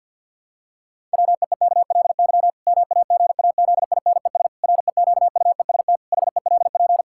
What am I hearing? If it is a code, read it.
Morse code, heard as a keyed tone, about 53 wpm, 700 Hz. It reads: WEEKLY MAGAZINES REQUEST HELP
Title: Pangram Challenge v2 - 50wpm